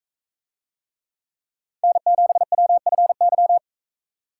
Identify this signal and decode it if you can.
Morse code, heard as a keyed tone, about 42 wpm, 700 Hz. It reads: N7WFY